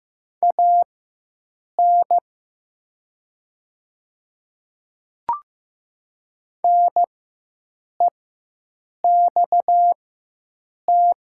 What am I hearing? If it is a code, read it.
Morse code, heard as a keyed tone, about 15 wpm, spaced as for 7 wpm, 700 Hz. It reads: AN NEXT
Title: 4x Character Spacing - Top 300 Words - 15wpm